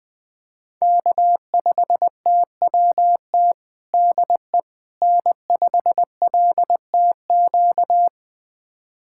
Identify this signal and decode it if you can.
Morse code, heard as a keyed tone, about 21 wpm, 700 Hz. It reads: K5TWT DE N5LTQ
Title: USA Callsign DE Callsign - 5 Characters - 20wpm